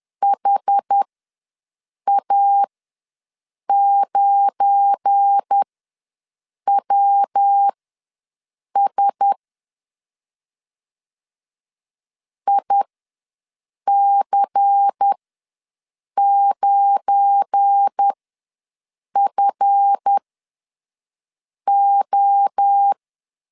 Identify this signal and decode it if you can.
Morse code, heard as a keyed tone, about 10 wpm, spaced as for 6 wpm, 800 Hz. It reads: HA9WS IC9FO
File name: LEZIONI / 7 GF9Z / 7 GF9 ws 10wpm 4034